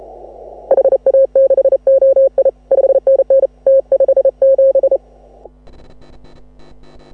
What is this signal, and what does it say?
Morse code, heard as a keyed tone, about 32 wpm, 540 Hz. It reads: HA6OI 5NN T57
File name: AUDIO2018 / HA6OI CW